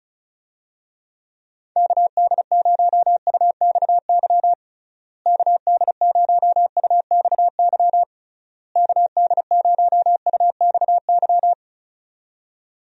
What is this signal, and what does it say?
Morse code, heard as a keyed tone, about 36 wpm, 700 Hz. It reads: KD0UXY KD0UXY KD0UXY
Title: USA Call Signs - 6 Characters - Repeated 3x - 35wpm